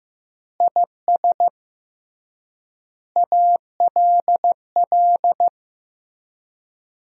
Morse code, heard as a keyed tone, about 15 wpm, 700 Hz. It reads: IS ALL
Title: 3x Word-Spacing - Sentences from Top 200 Words - 15wpm